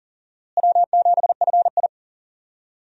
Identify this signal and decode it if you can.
Morse code, heard as a keyed tone, about 40 wpm, 700 Hz. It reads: W7FI
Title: USA Call Signs - 4 Characters - Repeated 3x - 40wpm